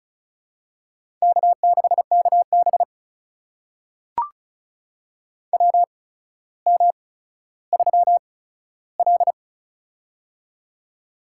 Morse code, heard as a keyed tone, about 36 wpm, spaced as for 15 wpm, 700 Hz. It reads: K6KB WM3L
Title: USA Call Signs - 4 Characters - 8x Character Spacing - 35wpm